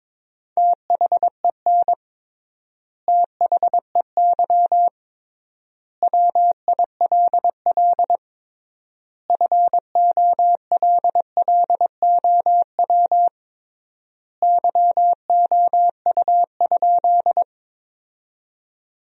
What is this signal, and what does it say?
Morse code, heard as a keyed tone, about 22 wpm, 700 Hz. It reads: THEN THEY WILL FOLLOW YOU?
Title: 3x Word-Spacing - Sentences from Top 200 Words - 22wpm